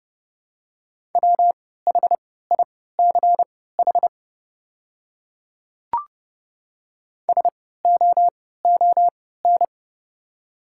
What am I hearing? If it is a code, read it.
Morse code, heard as a keyed tone, about 32 wpm, spaced as for 17 wpm, 700 Hz. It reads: WHICH SOON
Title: 3x Character Spacing - Top 500 Words - 30wpm